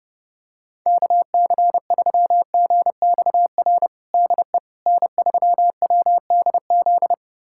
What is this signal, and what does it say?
Morse code, heard as a keyed tone, about 30 wpm, 700 Hz. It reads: KC3GXR DE N3WDZ
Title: USA Callsign DE Callsign 30wpm